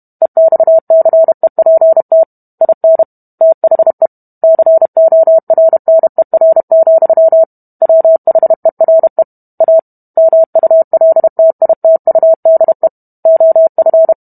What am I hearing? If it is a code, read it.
Morse code, heard as a keyed tone, about 32 wpm, 650 Hz. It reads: EXCEPT IN THE CORNER, WHERE A MULTITUDE OF